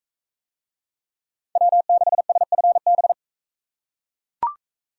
Morse code, heard as a keyed tone, about 42 wpm, 700 Hz. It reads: W6SFB